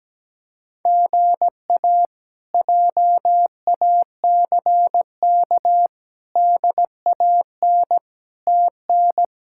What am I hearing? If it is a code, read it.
Morse code, heard as a keyed tone, about 17 wpm, 700 Hz. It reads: GA JACK DAN TN
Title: K1USN SST QSO Practice - 17wpm